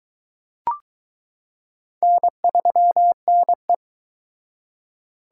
Morse code, heard as a keyed tone, about 21 wpm, 700 Hz. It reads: N3NE